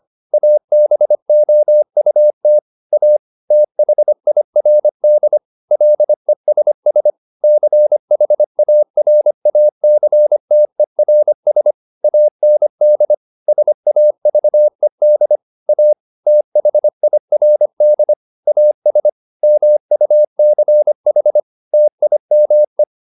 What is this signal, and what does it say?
Morse code, heard as a keyed tone, about 25 wpm, 600 Hz. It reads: ABOUT A THIRD LESS CHARACTERS AND SAVED A THIRD AS MUCH TIME